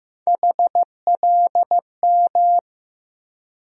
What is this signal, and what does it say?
Morse code, heard as a keyed tone, about 15 wpm, 700 Hz. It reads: HLM
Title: USA Call Sign Postfixes - 3 Letters 15wpm